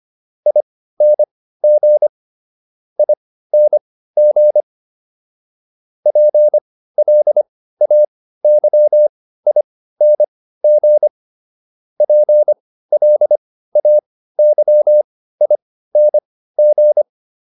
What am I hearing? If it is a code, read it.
Morse code, heard as a keyed tone, about 25 wpm, spaced as for 15 wpm, 600 Hz. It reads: ING ING PLAYING PLAYING